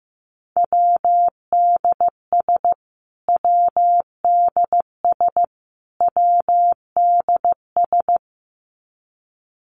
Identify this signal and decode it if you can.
Morse code, heard as a keyed tone, about 14 wpm, 700 Hz. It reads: WDS WDS WDS
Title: Abbreviations Repeated 3x - 15wpm